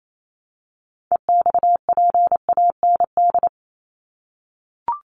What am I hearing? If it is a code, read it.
Morse code, heard as a keyed tone, about 28 wpm, 700 Hz. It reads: EXPAND
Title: Top 1251-1500 Words 28wpm